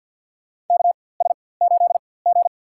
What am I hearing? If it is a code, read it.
Morse code, heard as a keyed tone, about 51 wpm, spaced as for 23 wpm, 700 Hz. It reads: KS8G
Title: USA Call Signs - 4x Character Spacing - 50wpm